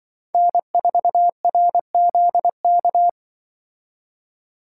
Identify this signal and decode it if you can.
Morse code, heard as a keyed tone, about 24 wpm, 700 Hz. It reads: N4RZK
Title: Speed Racing - USA 1x3 Call Signs 20wpm